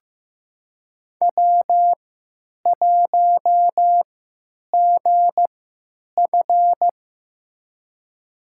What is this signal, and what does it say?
Morse code, heard as a keyed tone, about 15 wpm, spaced as for 8 wpm, 700 Hz. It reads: W1GF
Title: USA Call Signs - 4 Characters - 3x Character Spacing - 15wpm